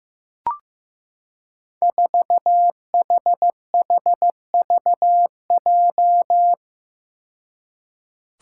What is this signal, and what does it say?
Morse code, heard as a keyed tone, about 15 wpm, 700 Hz. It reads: 4HHVJ
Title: Five Challenging Letter-Number Combinations 15wpm